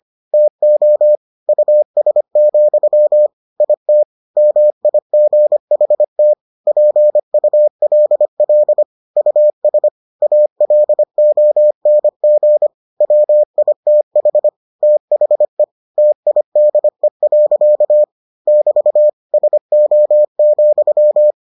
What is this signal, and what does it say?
Morse code, heard as a keyed tone, about 25 wpm, 600 Hz. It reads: TO US, IT MIGHT PULL US ALONG WITH THE TIDE. <BT> SO,